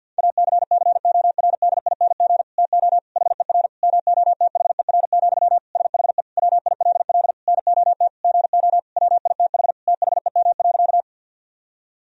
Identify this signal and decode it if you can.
Morse code, heard as a keyed tone, about 50 wpm, 700 Hz. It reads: ACCORDING TO HER MOTHER, SHE WILL NOT GO WITH THEM.
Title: Sentences from Top 1000 Words - Review Part 4 - 50wpm